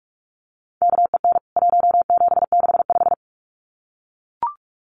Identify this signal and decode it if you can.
Morse code, heard as a keyed tone, about 44 wpm, 700 Hz. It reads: KEN 1765
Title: CWT Exchange 45wpm